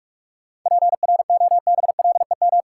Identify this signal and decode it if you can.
Morse code, heard as a keyed tone, about 42 wpm, 700 Hz. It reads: PROBLEM
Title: Sentences from Top 1000 Words - New Word Drill - 45wpm